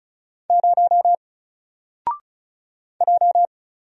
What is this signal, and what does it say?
Morse code, heard as a keyed tone, about 36 wpm, 700 Hz. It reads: 0 J
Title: Single Letter-Number 35wpm